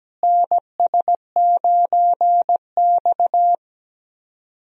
Morse code, heard as a keyed tone, about 17 wpm, 700 Hz. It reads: NS9X
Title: USA Call Signs - 4 Characters - Repeated 2x - 17wpm